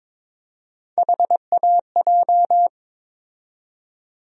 Morse code, heard as a keyed tone, about 22 wpm, 700 Hz. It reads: HAJ
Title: USA Call Sign Postfixes - 3 Letters 22wpm